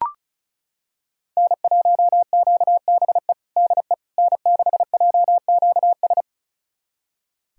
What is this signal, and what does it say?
Morse code, heard as a keyed tone, about 34 wpm, 700 Hz. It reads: N1QBE DE N6JQS
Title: USA Callsign DE Callsign - 5 Characters - 35wpm